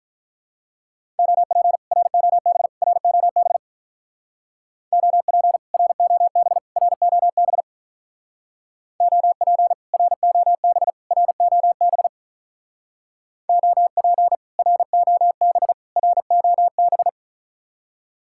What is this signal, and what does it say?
Morse code, heard as a keyed tone, about 44 wpm, 700 Hz. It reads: OP ROB ROB OP ROB ROB OP ROB ROB OP ROB ROB